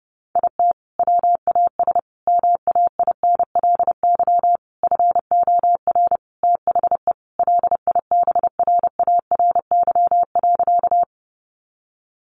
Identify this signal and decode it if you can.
Morse code, heard as a keyed tone, about 30 wpm, 700 Hz. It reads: IT WAS MAINLY FOR THE LIBRARY.
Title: Sentences from Top 1400 Words 30wpm